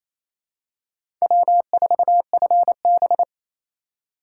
Morse code, heard as a keyed tone, about 28 wpm, 700 Hz. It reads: W4FB
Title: USA Call Signs - 4 Characters - 3x Character Spacing - 28wpm